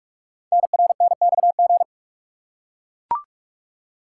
Morse code, heard as a keyed tone, about 44 wpm, 700 Hz. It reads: NRNXG
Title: Code Groups - 5 Letters - 45wpm